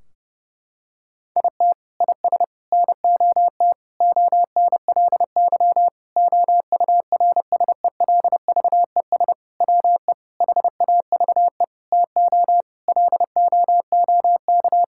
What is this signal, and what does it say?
Morse code, heard as a keyed tone, about 28 wpm, 700 Hz. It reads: IT IS NOT ONLY OURSELVES WE HAVE TO LOOK
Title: Sentences from Top 1600 Words 30wpm